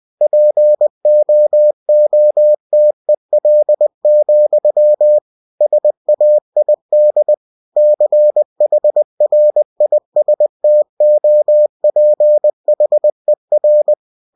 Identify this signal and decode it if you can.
Morse code, heard as a keyed tone, about 20 wpm, 600 Hz. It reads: POOTEL, SAID CHRISTOPHER
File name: cw-books / winnie-the-pooh / winnie-20wpm-page-0074